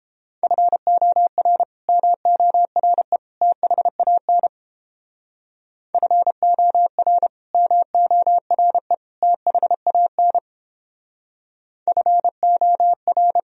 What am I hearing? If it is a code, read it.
Morse code, heard as a keyed tone, about 29 wpm, 700 Hz. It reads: FOR MORE THAN FOR MORE THAN FOR